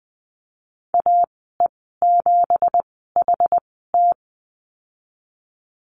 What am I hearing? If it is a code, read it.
Morse code, heard as a keyed tone, about 20 wpm, spaced as for 15 wpm, 700 Hz. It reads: AE7HT
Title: 167 Lesson 27g - Number 7 - Call Signs 20wpm